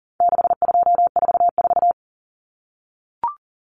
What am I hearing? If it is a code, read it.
Morse code, heard as a keyed tone, about 40 wpm, 700 Hz. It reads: 6244
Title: Numbers - 4-Digit 40wpm